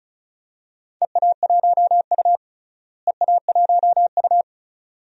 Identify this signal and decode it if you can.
Morse code, heard as a keyed tone, about 36 wpm, 700 Hz. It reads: EA1U EA1U
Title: DXCC-100 Call Signs - 4 Characters - Repeated 2x - 35wpm